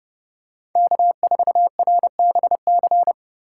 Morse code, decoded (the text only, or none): K4RBC